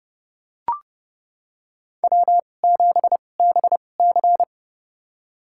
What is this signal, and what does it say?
Morse code, heard as a keyed tone, about 30 wpm, spaced as for 22 wpm, 700 Hz. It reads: W7BC
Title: USA Call Signs - 4 Characters - 2x Character Spacing - 30wpm